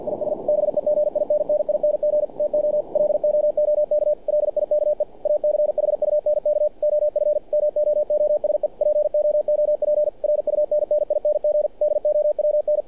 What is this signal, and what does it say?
Morse code, heard as a keyed tone, about 48 wpm, 600 Hz. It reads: IS BEGINNING TO LOOK LIKE A CHUNK OF MOOSE POOP RUNNING DOWN
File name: archived / CRSnet